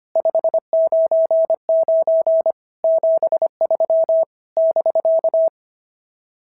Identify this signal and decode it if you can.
Morse code, heard as a keyed tone, about 22 wpm, 650 Hz. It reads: 599 73 <BK>
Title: SOTA QSO Practice 25wpm